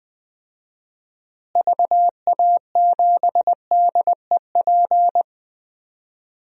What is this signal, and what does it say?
Morse code, heard as a keyed tone, about 20 wpm, 700 Hz. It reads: VA7DEP
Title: DXCC-100 Call Signs - 6 Characters - Repeated 3x - 20wpm